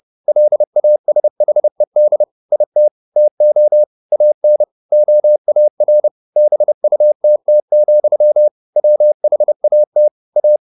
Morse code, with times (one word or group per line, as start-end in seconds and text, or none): 0.3-2.2 LASHED
2.5-2.9 IT
3.2-3.8 TO
4.1-4.6 AN
4.9-6.1 OAR
6.4-8.5 BUTT,
8.8-10.1 WHAT
10.4-10.6 A